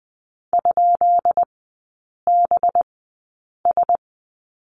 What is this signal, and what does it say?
Morse code, heard as a keyed tone, about 20 wpm, spaced as for 8 wpm, 700 Hz. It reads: ?BS
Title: 191 Lesson 31e - Letter v - Three Characters 20wpm